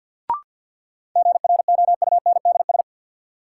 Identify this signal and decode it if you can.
Morse code, heard as a keyed tone, about 52 wpm, 700 Hz. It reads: GROUNDS
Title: Sentences from Top 1300 Words - New Word Drill Alt - 50wpm